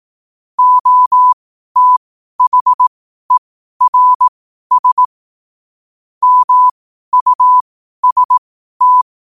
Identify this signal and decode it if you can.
Morse code, heard as a keyed tone, about 18 wpm, spaced as for 12 wpm, 1000 Hz. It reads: OTHERS MUST